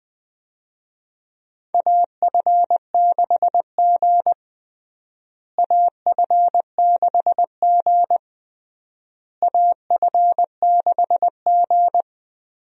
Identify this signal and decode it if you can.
Morse code, heard as a keyed tone, about 20 wpm, 700 Hz. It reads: AF6G AF6G AF6G
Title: USA Call Signs - 4 Characters - Repeated 3x - 20wpm